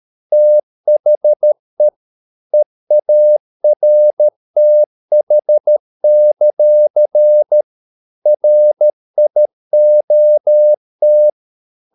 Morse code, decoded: THE EARTH; RIOT

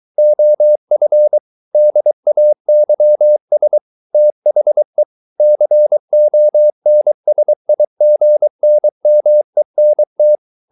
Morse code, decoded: OF DAYS THE CONSIGNMENT

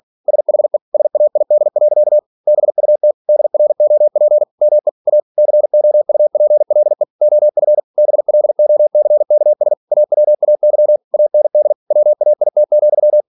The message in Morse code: SHE SAID. BUT DROP ME A COUPLE OF BLOCKS AWAY AND WAIT,

